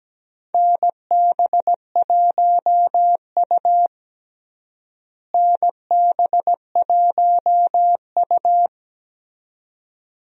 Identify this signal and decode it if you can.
Morse code, heard as a keyed tone, about 17 wpm, 700 Hz. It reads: NB1U NB1U
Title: USA Call Signs - 4 Characters - Repeated 2x - 17wpm